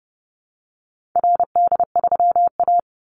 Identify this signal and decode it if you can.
Morse code, heard as a keyed tone, about 30 wpm, 700 Hz. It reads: RD3A
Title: Call Signs Repeated 2x - 30wpm